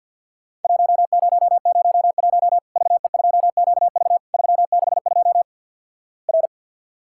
Morse code, heard as a keyed tone, about 50 wpm, 700 Hz. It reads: 1001 VE2QV 362 R
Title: Worked All Europe DX Contest - QTC Exchange with Repeats - 50wpm